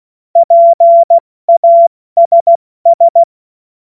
Morse code, heard as a keyed tone, about 16 wpm, 680 Hz. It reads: PASS
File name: Morsekurs / Witze / Witz 13 Mai 2025 70 F80